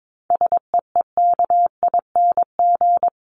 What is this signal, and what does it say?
Morse code, heard as a keyed tone, about 22 wpm, 700 Hz. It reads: SEEKING
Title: ING Suffix - Top 100 Words with Base Words 22wpm